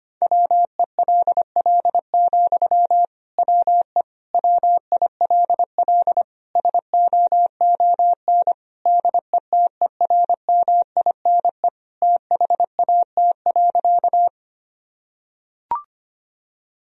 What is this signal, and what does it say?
Morse code, heard as a keyed tone, about 25 wpm, 700 Hz. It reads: WELL, WE WILL SOON DETERMINE THAT.
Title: Sentences from Top 1000 Words - Review Part 4 - 25wpm